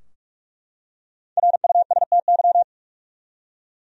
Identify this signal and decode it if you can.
Morse code, heard as a keyed tone, about 44 wpm, 700 Hz. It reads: RUSTY
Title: USA Common Names - Max 5 Letters - 45wpm